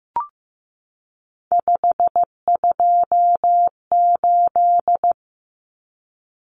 Morse code, heard as a keyed tone, about 15 wpm, 700 Hz. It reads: 528
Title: Common Cut Numbers - 3-Digit 15wpm